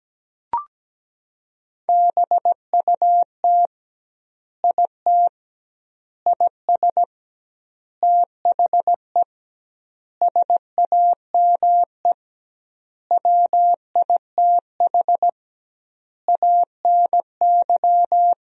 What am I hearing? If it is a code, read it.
Morse code, heard as a keyed tone, about 17 wpm, 700 Hz. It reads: BUT IT IS THE SAME WITH ANY
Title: 2x Word-Spacing - Sentences from Top 300 Words - 17wpm